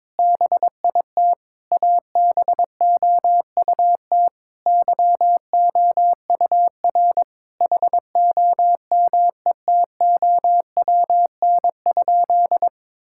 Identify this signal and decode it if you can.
Morse code, heard as a keyed tone, about 22 wpm, 700 Hz. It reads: BIT ABOUT YOUR HOMETOWN?
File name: practice / 22wpm / Sentences with 8-Word Phrases - 22wpm